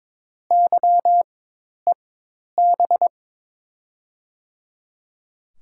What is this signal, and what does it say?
Morse code, heard as a keyed tone, about 21 wpm, spaced as for 10 wpm, 700 Hz. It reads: YEB
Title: USA Call Signs - 5 Characters - 4x Character Spacing - 22wpm